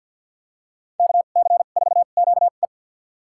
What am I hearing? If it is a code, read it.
Morse code, heard as a keyed tone, about 46 wpm, spaced as for 35 wpm, 700 Hz. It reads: KC4QE
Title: USA Call Signs - 5 Characters - 2x Character Spacing - 50wpm